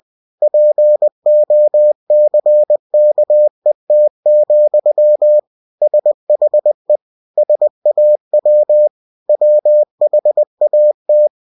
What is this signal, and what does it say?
Morse code, heard as a keyed tone, about 20 wpm, 600 Hz. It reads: POCKET, SHE SAW WHAT
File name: cw-books / winnie-the-pooh / winnie-20wpm-page-0071